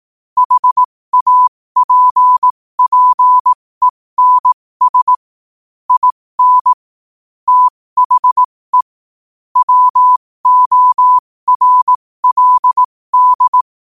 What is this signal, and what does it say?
Morse code, heard as a keyed tone, about 18 wpm, spaced as for 15 wpm, 1000 Hz. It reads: HAPPENS IN THE WORLD
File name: qotdfiles / 15wpm